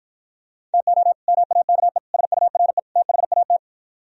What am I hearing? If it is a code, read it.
Morse code, heard as a keyed tone, about 55 wpm, 700 Hz. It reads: TO MAKE SURE THAT